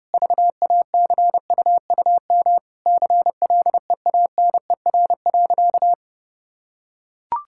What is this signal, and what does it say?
Morse code, heard as a keyed tone, about 30 wpm, 700 Hz. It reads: VACUUM CLEANER.
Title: Pangram Challenge v2 - 30wpm